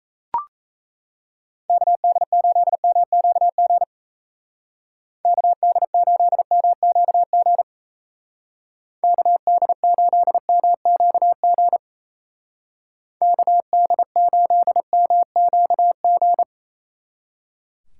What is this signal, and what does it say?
Morse code, heard as a keyed tone, about 34 wpm, 700 Hz. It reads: KD8MQG KD8MQG KD8MQG KD8MQG
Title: Speed Racing - USA 2x3 Call Signs 28wpm